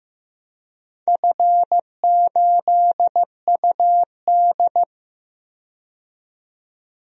Morse code, heard as a keyed tone, about 15 wpm, 700 Hz. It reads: F8UD
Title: DXCC-100 Call Signs - 4 Characters - 15wpm